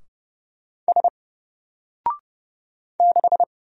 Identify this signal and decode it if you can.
Morse code, heard as a keyed tone, about 31 wpm, 700 Hz. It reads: S 6
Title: Single Letter-Number Alt 30wpm